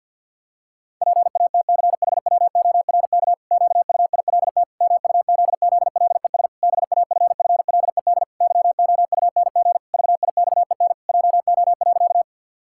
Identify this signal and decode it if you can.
Morse code, heard as a keyed tone, about 50 wpm, 700 Hz. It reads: PATCHWORK QUILT MUZZLES BAFFLED YOUNG VIXEN JO.
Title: Pangram Challenge v2 - 50wpm